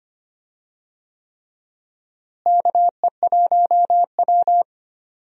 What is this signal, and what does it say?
Morse code, heard as a keyed tone, about 25 wpm, 700 Hz. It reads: KE1W
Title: USA Call Signs - 4 Characters - Repeated 3x - 25wpm